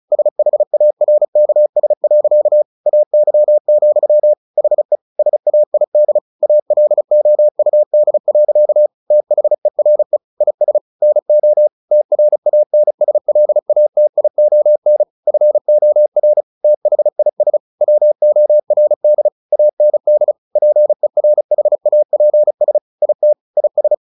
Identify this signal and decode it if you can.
Morse code, heard as a keyed tone, about 34 wpm, 600 Hz. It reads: SHARKS. AY, HE SAID ALOUD. THERE IS NO TRANSLATION FOR THIS WORD AND PERHAPS IT IS
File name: the old man and the sea 2 35wpm